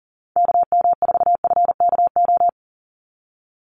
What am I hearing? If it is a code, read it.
Morse code, heard as a keyed tone, about 40 wpm, 700 Hz. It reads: KM4FKO
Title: USA Call Signs 40wpm